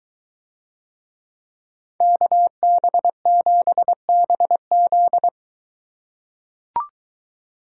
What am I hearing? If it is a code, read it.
Morse code, heard as a keyed tone, about 22 wpm, 700 Hz. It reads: KB7BZ